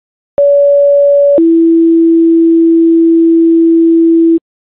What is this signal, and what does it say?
Two-tone page, heard as a one-second tone, then a three-second tone, about 570 then 330 Hz. It is heard from 0.4 to 4.4 s.